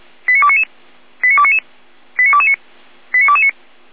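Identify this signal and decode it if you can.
Five-tone selective call (ZVEI), heard as quick runs of five tones, tones of about 70 ms. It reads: 89299 89299 89229 89229
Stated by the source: Five-tone ZVEI selective call received on an SDR.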